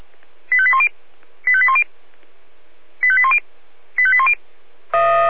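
Five-tone selective call (ZVEI), heard as quick runs of five tones, tones of about 70 ms. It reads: 85810 85810 85810 85810
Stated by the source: ZVEI five-tone selective call received via SDR.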